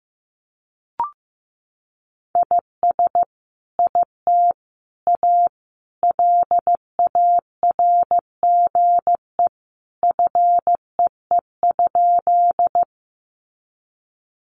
Morse code, heard as a keyed tone, about 15 wpm, 700 Hz. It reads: IS IT A LARGE FEE?
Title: Sentences from Top 1500 Words 15wpm